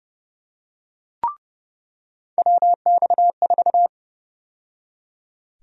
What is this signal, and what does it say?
Morse code, heard as a keyed tone, about 30 wpm, 700 Hz. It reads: WX4
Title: USA Call Sign Prefixes - 30wpm